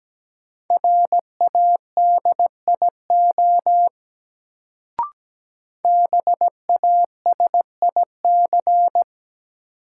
Morse code, heard as a keyed tone, about 17 wpm, 700 Hz. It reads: RADIO BASIC
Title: Sentences from Top 1000 Words - Review Part 3 - New Word Drill - 17wpm